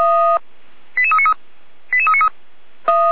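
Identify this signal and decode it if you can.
Five-tone selective call (ZVEI), heard as quick runs of five tones, tones of about 70 ms. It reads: 88393 88393
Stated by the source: ZVEI five-tone call 88327 received on SDR.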